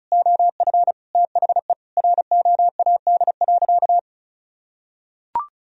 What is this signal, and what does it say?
Morse code, heard as a keyed tone, about 34 wpm, 700 Hz. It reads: OF THE ROAD.
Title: Sentences from Top 400 Words and Abbreviations 35wpm